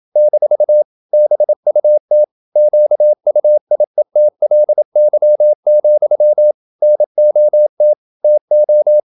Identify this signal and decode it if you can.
Morse code, heard as a keyed tone, about 27 wpm, 600 Hz. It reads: - BUT QUIETLY, NOT TO
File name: Never 0027wpm